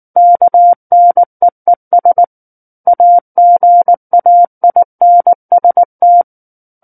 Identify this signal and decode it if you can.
Morse code, heard as a keyed tone, about 19 wpm, 700 Hz. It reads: KNEES AGAINST
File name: Chapter 10 - 19 WPM